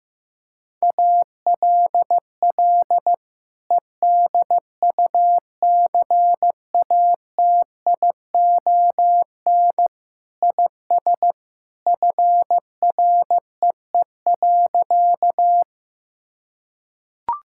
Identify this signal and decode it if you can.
Morse code, heard as a keyed tone, about 16 wpm, 700 Hz. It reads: ALL EDUCATION IS FREE.
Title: Sentences from Top 1000 Words - Review Part 6 - 15wpm